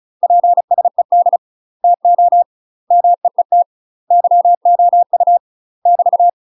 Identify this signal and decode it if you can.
Morse code, heard as a keyed tone, about 34 wpm, 700 Hz. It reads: PSED TO MEET YOU <BT>
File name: UK MASTER QSOs TRAINING/35WPM